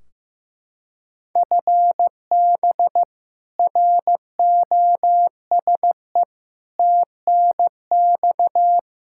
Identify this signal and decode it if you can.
Morse code, heard as a keyed tone, about 15 wpm, 700 Hz. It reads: FB ROSE TNX
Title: Practice QSO - Exchange 3 - More Info - 15wpm